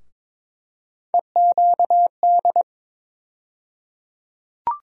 Morse code, heard as a keyed tone, about 22 wpm, 700 Hz. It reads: EQD
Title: USA Call Sign Postfixes - 3 Letters 22wpm